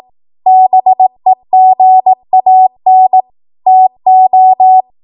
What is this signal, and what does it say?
Morse code, heard as a keyed tone, about 18 wpm, 750 Hz. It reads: BEGAN TO
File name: Archive / 18 WPM / 18WPM